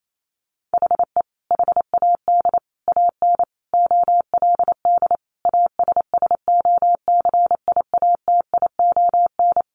Morse code, heard as a keyed tone, about 28 wpm, 700 Hz. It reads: HE HAD AN OLD ASSOCIATION
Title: Sentences from Top 1100 Words 28wpm